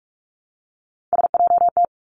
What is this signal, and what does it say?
Morse code, heard as a keyed tone, about 46 wpm, 700 Hz. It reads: SJT
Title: Three Challenging Letter-Number Combinations 45wpm